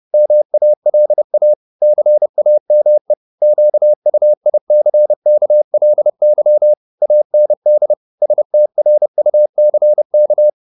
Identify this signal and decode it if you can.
Morse code, heard as a keyed tone, about 30 wpm, 600 Hz. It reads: MALA CAME QUICKLY AND STRUCK
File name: the old man and the sea / the old man and the sea 1 30wpm